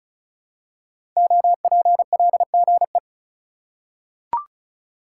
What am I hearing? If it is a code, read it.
Morse code, heard as a keyed tone, about 34 wpm, 700 Hz. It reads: OPLGE
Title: Code Groups - 5 Letters - 35wpm